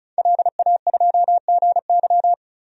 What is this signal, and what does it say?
Morse code, heard as a keyed tone, about 36 wpm, 700 Hz. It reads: LA2GY
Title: DXCC-100 Call Signs - 5 Characters - 35wpm